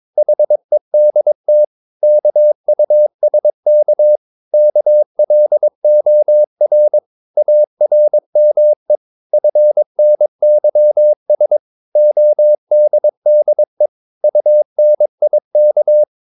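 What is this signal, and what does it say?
Morse code, heard as a keyed tone, about 22 wpm, 600 Hz. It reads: HEDT KUSK KLOR ARME FNYS ODDE UNIK
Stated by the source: Morse practice of short Danish words at 22 wpm, from CW med Gnister.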